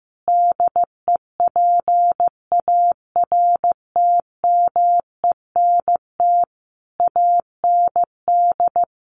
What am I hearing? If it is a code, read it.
Morse code, heard as a keyed tone, about 15 wpm, 700 Hz. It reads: DEPARTMENT AND
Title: Sentences from Top 1400 Words 15wpm